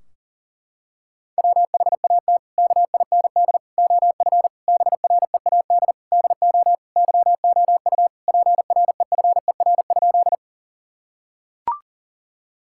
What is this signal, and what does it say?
Morse code, heard as a keyed tone, about 40 wpm, 700 Hz. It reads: WHAT KIND OF BREAD DO YOU PREFER?
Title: Sentences from Top 1600 Words 40wpm